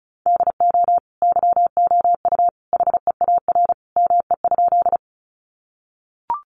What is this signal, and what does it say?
Morse code, heard as a keyed tone, about 34 wpm, 700 Hz. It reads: DO YOU HEAR ME?
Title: Sentences from Top 300 Words 35wpm